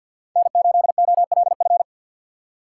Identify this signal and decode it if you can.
Morse code, heard as a keyed tone, about 52 wpm, 700 Hz. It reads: N8OLF
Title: USA Call Signs - 5 Characters - 4x Character Spacing - 50wpm